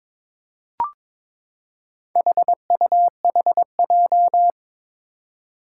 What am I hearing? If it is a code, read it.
Morse code, heard as a keyed tone, about 22 wpm, 700 Hz. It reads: HUHJ
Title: Four Challenging Letter-Number Combinations 22wpm